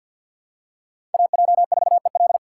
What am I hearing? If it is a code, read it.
Morse code, heard as a keyed tone, about 51 wpm, 700 Hz. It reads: AJ4EL
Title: USA Call Signs - 5 Characters - 2x Character Spacing - 50wpm